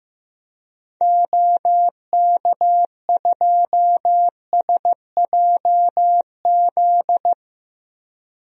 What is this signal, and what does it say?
Morse code, heard as a keyed tone, about 15 wpm, 700 Hz. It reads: OK2SJZ
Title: DXCC-100 Call Signs - 6 Characters - 15wpm